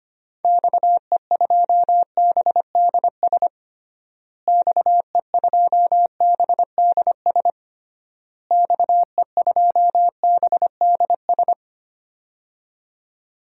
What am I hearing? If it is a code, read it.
Morse code, heard as a keyed tone, about 25 wpm, 700 Hz. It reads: XE2BDS XE2BDS XE2BDS
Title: DXCC-100 Call Signs - 6 Characters - Repeated 3x - 25wpm